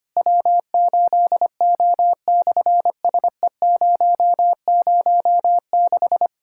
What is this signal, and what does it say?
Morse code, heard as a keyed tone, about 25 wpm, 700 Hz. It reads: W8O/SE006
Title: SOTA QSO Practice 25wpm